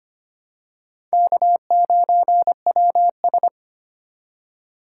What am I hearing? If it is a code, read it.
Morse code, heard as a keyed tone, about 25 wpm, 700 Hz. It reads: K9WS
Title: USA Call Signs - 4 Characters - 2x Character Spacing - 25wpm